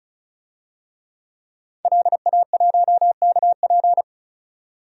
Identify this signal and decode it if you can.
Morse code, heard as a keyed tone, about 34 wpm, 700 Hz. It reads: LA1KP
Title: DXCC-100 Call Signs - 5 Characters - 35wpm